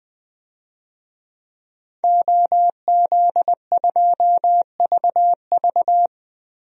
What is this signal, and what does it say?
Morse code, heard as a keyed tone, about 21 wpm, 700 Hz. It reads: OZ2VV
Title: DXCC-100 Call Signs - 5 Characters - Repeated 2x - 20wpm